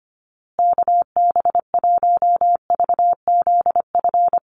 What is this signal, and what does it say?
Morse code, heard as a keyed tone, about 25 wpm, 700 Hz. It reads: KB1VZF
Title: USA Call Signs Repeated 2x 25wpm